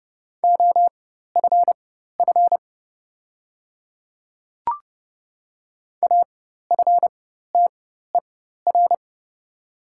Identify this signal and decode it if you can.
Morse code, heard as a keyed tone, about 30 wpm, spaced as for 14 wpm, 700 Hz. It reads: OFF AFTER
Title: 4x Character Spacing - Top 200 Words - 30wpm